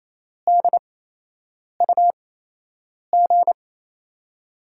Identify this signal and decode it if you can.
Morse code, heard as a keyed tone, about 27 wpm, spaced as for 8 wpm, 700 Hz. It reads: DUG
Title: USA Call Signs - 5 Characters - 8x Character Spacing - 28wpm